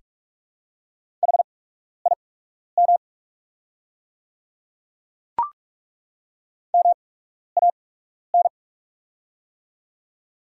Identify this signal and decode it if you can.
Morse code, heard as a keyed tone, about 42 wpm, spaced as for 12 wpm, 690 Hz. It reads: HIM MAN
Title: 8x Character Spacing - Top 300 Words - 45wpm